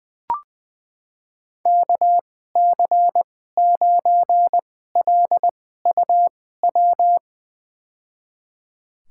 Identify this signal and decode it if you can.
Morse code, heard as a keyed tone, about 20 wpm, spaced as for 14 wpm, 700 Hz. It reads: KC9LUW
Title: USA Call Signs - 6 Characters - 2x Character Spacing - 20wpm